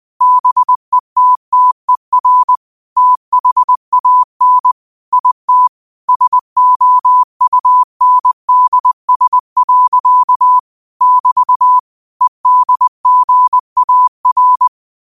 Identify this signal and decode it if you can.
Morse code, heard as a keyed tone, about 20 wpm, 1000 Hz. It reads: BETTER THAN IT SOUNDS. <BT> EDGAR
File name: qotdfiles / 20wpm